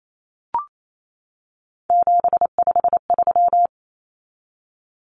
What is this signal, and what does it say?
Morse code, heard as a keyed tone, about 28 wpm, 700 Hz. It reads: 753
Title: Common Cut Numbers - 3-Digit 28wpm